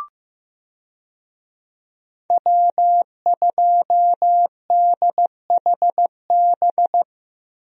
Morse code, heard as a keyed tone, about 15 wpm, 700 Hz. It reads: W2DHB
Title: USA Call Signs - 5 Characters - Repeated 3x - 15wpm